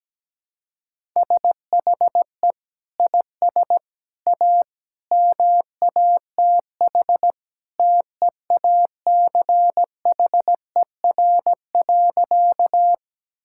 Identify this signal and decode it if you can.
Morse code, heard as a keyed tone, about 17 wpm, 700 Hz. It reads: SHE IS A MATH TEACHER.
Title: Sentences from Top 1000 Words - Review Part 6 - 17wpm